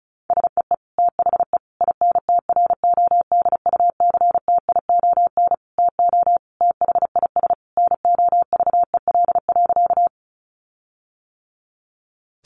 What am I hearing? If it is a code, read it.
Morse code, heard as a keyed tone, about 34 wpm, 700 Hz. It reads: SEE THE INTRODUCTION TO THIS NOVEL.
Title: Sentences from Top 1300 Words 35wpm